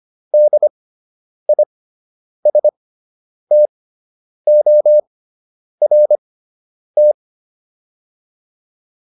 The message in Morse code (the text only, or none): DISTORT